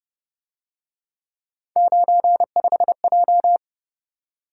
Morse code, heard as a keyed tone, about 26 wpm, 700 Hz. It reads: 95J
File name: Speed-Racing - Three Letter-Number Combinations - 22wpm